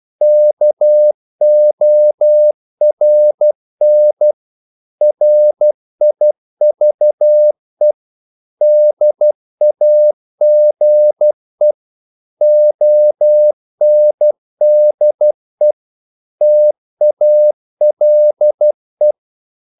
Morse code, heard as a keyed tone, about 12 wpm, 600 Hz. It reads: KORN RIVE DAGE ONDE TALE
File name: Korte ord DK 12wpm